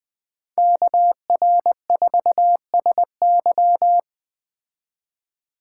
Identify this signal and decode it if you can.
Morse code, heard as a keyed tone, about 20 wpm, 700 Hz. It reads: KR4SY
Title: Speed Racing - USA Call Signs 17wpm